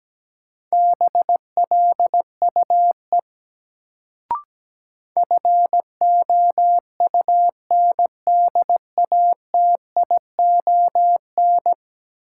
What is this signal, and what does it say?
Morse code, heard as a keyed tone, about 17 wpm, 700 Hz. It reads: BLUE FOUNDATION
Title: Sentences from Top 1200 Words - New Word Drill Alt - 17wpm